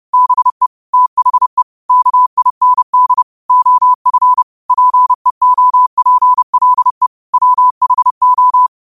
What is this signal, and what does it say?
Morse code, heard as a keyed tone, about 30 wpm, 1000 Hz. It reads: BE THE KIND OF PEOPLE WHO